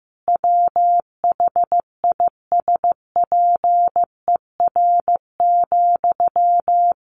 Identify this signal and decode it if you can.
Morse code, heard as a keyed tone, about 15 wpm, 700 Hz. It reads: WHISPER,
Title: A Sound Of Thunder 15wpm